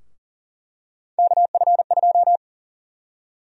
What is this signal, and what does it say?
Morse code, heard as a keyed tone, about 40 wpm, 700 Hz. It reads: KF2
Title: USA Call Sign Prefixes - Repeated 2x - 40wpm